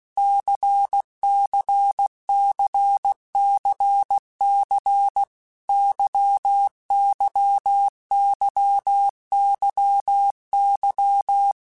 Morse code, heard as a keyed tone, about 16 wpm, 800 Hz. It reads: CCCCC YYYYY
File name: Basic Letter Groups By7Groups / LFCYQ / LFCYQ@15